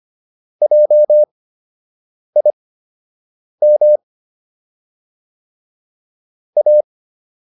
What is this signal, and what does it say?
Morse code, heard as a keyed tone, about 25 wpm, spaced as for 7 wpm, 600 Hz. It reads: JIM A